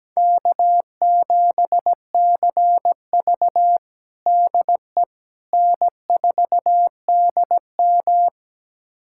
Morse code, heard as a keyed tone, about 17 wpm, 700 Hz. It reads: K7CV DE N4DM
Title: USA Callsign DE Callsign - 4 Characters - 17wpm